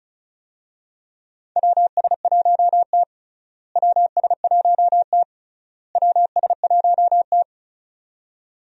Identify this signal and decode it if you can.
Morse code, heard as a keyed tone, about 36 wpm, 700 Hz. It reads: WS1T WS1T WS1T
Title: USA Call Signs - 4 Characters - Repeated 3x - 35wpm